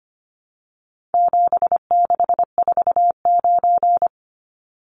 Morse code, heard as a keyed tone, about 25 wpm, 700 Hz. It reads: 7649
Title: Numbers - 4-Digit 25wpm